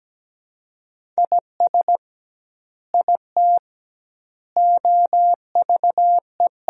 Morse code, heard as a keyed tone, about 17 wpm, 700 Hz. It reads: IS IT OVE
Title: 2x Word-Spacing - Sentences from Top 100 Words - 17wpm